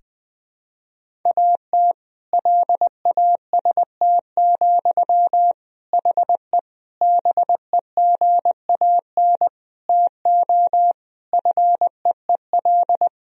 Morse code, heard as a keyed tone, about 20 wpm, 700 Hz. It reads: AT LAST, HE BEGAN TO FEEL
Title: Sentences from Top 1000 Words - Review Part 1 - 20wpm